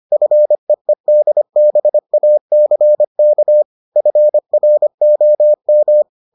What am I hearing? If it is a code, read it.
Morse code, heard as a keyed tone, about 25 wpm, 600 Hz. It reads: FEEDBACK FROM